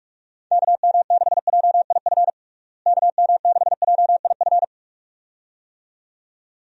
Morse code, heard as a keyed tone, about 47 wpm, 700 Hz. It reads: KM6JIF KM6JIF
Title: USA Call Signs - 6 Characters - Repeated 2x - 45wpm